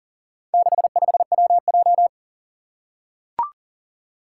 Morse code, heard as a keyed tone, about 40 wpm, 700 Hz. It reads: 65WJ
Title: Four Challenging Letter-Number Combinations 40wpm